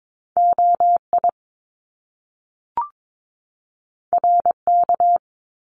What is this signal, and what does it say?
Morse code, heard as a keyed tone, about 22 wpm, 700 Hz. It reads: OI RK
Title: Top Two Letter Combinations in Words 22wpm